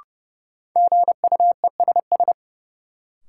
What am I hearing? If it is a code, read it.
Morse code, heard as a keyed tone, about 30 wpm, 700 Hz. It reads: GUESS
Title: Sentences from Top 1500 Words - New Word Drill - 30wpm